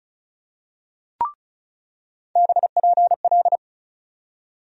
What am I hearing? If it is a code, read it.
Morse code, heard as a keyed tone, about 36 wpm, 700 Hz. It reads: BPL
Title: USA Call Sign Postfixes - 3 Letters 35wpm